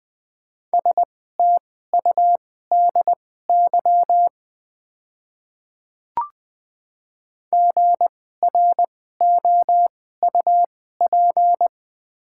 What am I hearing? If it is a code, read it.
Morse code, heard as a keyed tone, about 21 wpm, spaced as for 15 wpm, 700 Hz. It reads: STUDY GROUP
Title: 2x Character Spacing - Top 500 Words - 20wpm